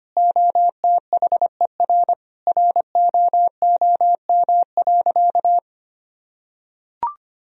Morse code, heard as a keyed tone, about 25 wpm, 700 Hz. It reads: OTHER ROOM.
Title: Sentences from Top 1000 Words - Review Part 1 - 25wpm